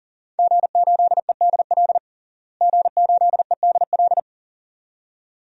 Morse code, heard as a keyed tone, about 40 wpm, 700 Hz. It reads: G8EDL G8EDL
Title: DXCC-100 Call Signs - 5 Characters - Repeated 2x - 40wpm